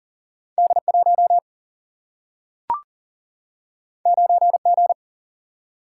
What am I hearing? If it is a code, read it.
Morse code, heard as a keyed tone, about 40 wpm, 700 Hz. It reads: D1 9G
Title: Single-Letter Single-Number 40wpm